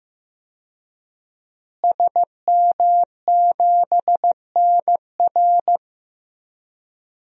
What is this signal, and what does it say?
Morse code, heard as a keyed tone, about 15 wpm, 700 Hz. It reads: SM7NR
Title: DXCC-100 Call Signs - 5 Characters - Repeated 3x - 15wpm